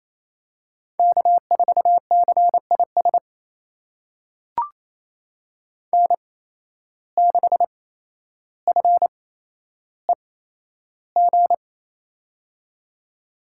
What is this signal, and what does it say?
Morse code, heard as a keyed tone, about 28 wpm, spaced as for 12 wpm, 700 Hz. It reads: K4CIS N6FEG
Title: USA Call Signs - 5 Characters - 8x Character Spacing - 28wpm